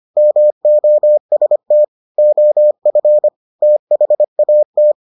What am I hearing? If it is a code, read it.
Morse code, heard as a keyed tone, about 25 wpm, 600 Hz. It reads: MOST OF THAT